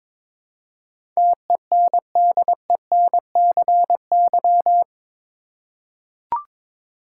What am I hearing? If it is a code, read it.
Morse code, heard as a keyed tone, about 22 wpm, 700 Hz. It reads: TENDENCY
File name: practice / 22wpm / Eight Letter Words - 22wpm